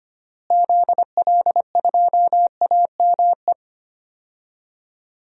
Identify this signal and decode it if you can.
Morse code, heard as a keyed tone, about 25 wpm, 700 Hz. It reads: ZL2AME
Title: DXCC-100 Call Signs - 6 Characters - Repeated 2x - 25wpm